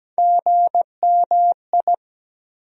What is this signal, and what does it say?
Morse code, heard as a keyed tone, about 17 wpm, 700 Hz. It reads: GMI
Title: USA Call Sign Postfixes - 3 Letters 17wpm